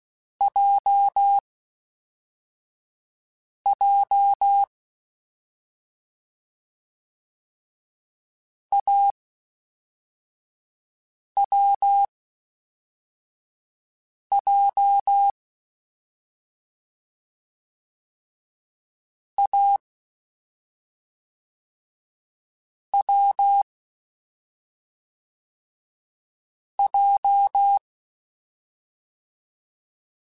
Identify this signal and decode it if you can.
Morse code, heard as a keyed tone, about 16 wpm, spaced as for 4 wpm, 800 Hz. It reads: JJ AWJ A W J
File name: Difficult Combos / Difficult Combos-mp3s / AWJ @5